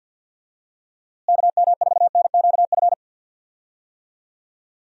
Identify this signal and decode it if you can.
Morse code, heard as a keyed tone, about 50 wpm, 700 Hz. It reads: KM4NQF
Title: USA Call Signs - 6 Characters - Repeated 3x - 50wpm